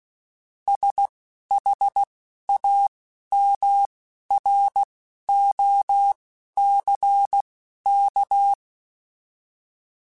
Morse code, heard as a keyed tone, about 16 wpm, spaced as for 11 wpm, 800 Hz. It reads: SHAMROCK